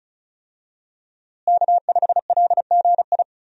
Morse code, heard as a keyed tone, about 34 wpm, 700 Hz. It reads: K5LGI
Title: USA Call Signs - 5 Characters - Repeated 3x - 35wpm